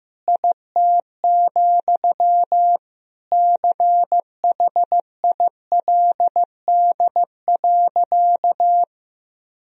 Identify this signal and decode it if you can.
Morse code, heard as a keyed tone, about 15 wpm, 700 Hz. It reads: IT, CHILD.
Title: Sentences from Top 1000 Words - Review Part 1 - 15wpm